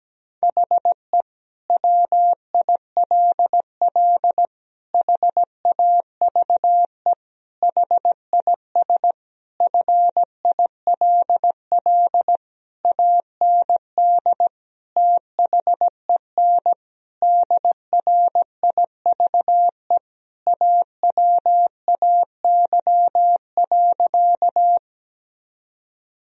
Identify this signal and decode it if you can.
Morse code, heard as a keyed tone, about 17 wpm, 700 Hz. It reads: HE WILL HAVE HIS FILL AND THEN DRIVE AWAY.
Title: Sentences from Top 1000 Words - Review Part 3 - 17wpm